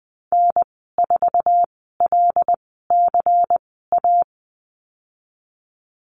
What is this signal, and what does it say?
Morse code, heard as a keyed tone, about 20 wpm, spaced as for 15 wpm, 700 Hz. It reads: N4LCA
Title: 122 Lesson 20g - Number 3 - Call Signs 20wpm